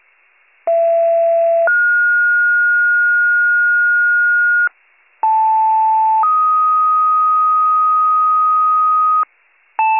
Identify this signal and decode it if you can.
Two-tone page, heard as a one-second tone, then a three-second tone, about 670 then 1430 Hz, about 880 then 1250 Hz, with noise. It is heard from 0.7 to 9.2 s.